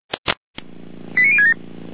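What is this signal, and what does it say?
Five-tone selective call (ZVEI), heard as quick runs of five tones, tones of about 70 ms. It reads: 89067
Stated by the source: ZVEI five-tone call received by SDR, from the fire brigade Feuerwehr Orlamünde server.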